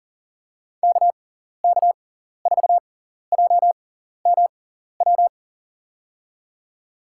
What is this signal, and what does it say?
Morse code, heard as a keyed tone, about 40 wpm, spaced as for 14 wpm, 700 Hz. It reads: KK4JMW
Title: USA Call Signs - 6 Characters - 6x Character Spacing - 40wpm